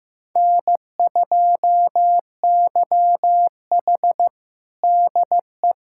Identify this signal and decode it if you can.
Morse code, heard as a keyed tone, about 15 wpm, 700 Hz. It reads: N2YH DE
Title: USA Callsign DE Callsign - 4 Characters - 15wpm